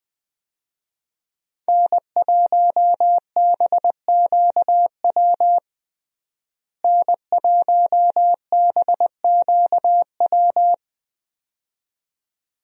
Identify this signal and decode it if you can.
Morse code, heard as a keyed tone, about 20 wpm, 700 Hz. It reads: N1BQW N1BQW
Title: USA Call Signs - 5 Characters - Repeated 2x - 20wpm